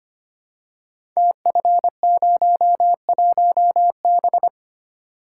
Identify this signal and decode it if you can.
Morse code, heard as a keyed tone, about 25 wpm, 700 Hz. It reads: TF01B
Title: Code Groups - 5 Letter:Numbers - 25wpm